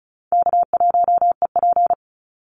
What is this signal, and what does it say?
Morse code, heard as a keyed tone, about 34 wpm, 700 Hz. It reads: K1EP
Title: Call Signs Repeated 3x - 35wpm